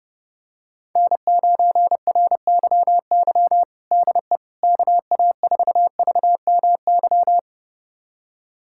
Morse code, heard as a keyed tone, about 30 wpm, 700 Hz. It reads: N9RYY DE KA4VMY
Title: USA Callsign DE Callsign 30wpm